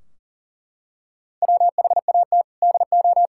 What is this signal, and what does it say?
Morse code, heard as a keyed tone, about 37 wpm, 700 Hz. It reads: WHAT DO
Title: Sentences from Top 1600 Words 40wpm